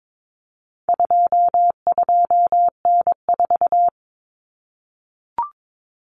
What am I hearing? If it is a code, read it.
Morse code, heard as a keyed tone, about 22 wpm, 700 Hz. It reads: 22N4
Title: Common Cut Numbers - 4-Digit 22wpm